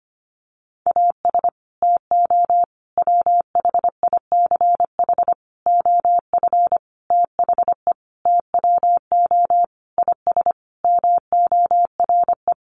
Morse code, heard as a keyed tone, about 25 wpm, 700 Hz. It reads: AS TO WHICH OF THE TWO IS MORE
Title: Sentences from Top 1200 Words 25wpm